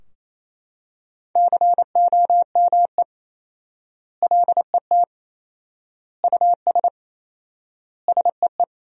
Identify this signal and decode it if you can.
Morse code, heard as a keyed tone, about 28 wpm, 700 Hz. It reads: COME LET US SEE
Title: 4x Word-Spacing - Sentences from Top 300 Words - 28wpm